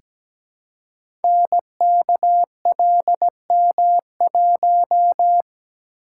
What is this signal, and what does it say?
Morse code, heard as a keyed tone, about 17 wpm, 700 Hz. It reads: NKLM1
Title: Code Groups - 5 Letter:Numbers - 17wpm